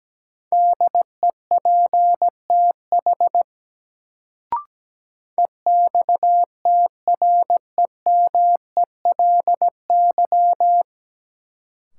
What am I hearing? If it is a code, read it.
Morse code, heard as a keyed tone, about 17 wpm, 700 Hz. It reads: DEPTH EXTREMELY
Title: Sentences from Top 1200 Words - New Word Drill Alt - 17wpm